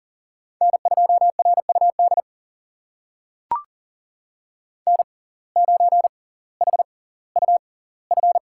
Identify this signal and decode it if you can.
Morse code, heard as a keyed tone, about 39 wpm, spaced as for 20 wpm, 700 Hz. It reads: N2RUD N9HUF
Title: USA Call Signs - 5 Characters - 6x Character Spacing - 40wpm